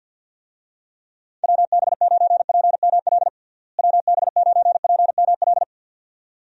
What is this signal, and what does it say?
Morse code, heard as a keyed tone, about 49 wpm, 700 Hz. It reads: WB9PML WB9PML
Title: USA Call Signs - 6 Characters - Repeated 2x - 50wpm